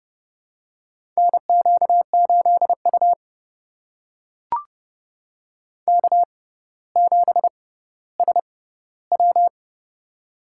Morse code, heard as a keyed tone, about 31 wpm, spaced as for 15 wpm, 700 Hz. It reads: NQ8U K7SW
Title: USA Call Signs - 4 Characters - 6x Character Spacing - 30wpm